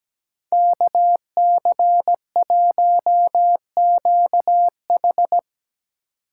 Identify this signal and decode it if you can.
Morse code, heard as a keyed tone, about 19 wpm, 700 Hz. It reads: KC1QH
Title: Speed Racing - USA Call Signs 17wpm